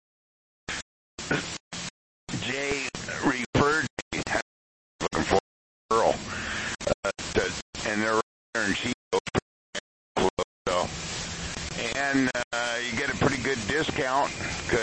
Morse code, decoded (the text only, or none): none